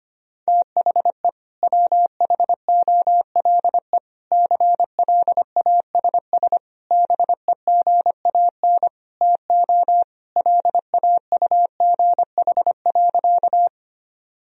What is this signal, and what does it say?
Morse code, heard as a keyed tone, about 25 wpm, 700 Hz. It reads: THE WHOLE CLASS BEGAN TO LAUGH.
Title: Sentences from Top 1000 Words - Review Part 2 - 25wpm